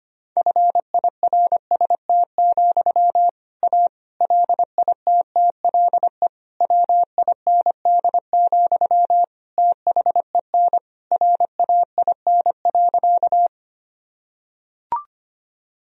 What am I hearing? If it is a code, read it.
Morse code, heard as a keyed tone, about 25 wpm, 700 Hz. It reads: FIRST, A LITTLE WIND, THEN RAIN.
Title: Sentences from Top 1000 Words - Review Part 2 - 25wpm